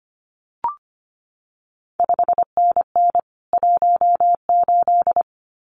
Morse code, heard as a keyed tone, about 25 wpm, 700 Hz. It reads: 5NN 18
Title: CQ WW DX Contest Exchange Encore - 25wpm